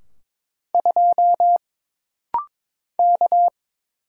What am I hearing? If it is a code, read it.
Morse code, heard as a keyed tone, about 22 wpm, 700 Hz. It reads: 2 K
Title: Single Letter-Number - Rapid-Fire 22wpm